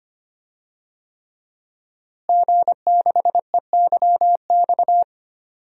Morse code, heard as a keyed tone, about 25 wpm, 700 Hz. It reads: G6EYX
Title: DXCC-100 Call Signs - 5 Characters - Repeated 3x - 25wpm